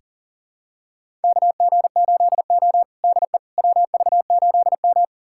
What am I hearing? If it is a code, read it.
Morse code, heard as a keyed tone, about 37 wpm, 700 Hz. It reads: KG8O DE WV8M
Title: USA Callsign DE Callsign - 4 Characters - 40wpm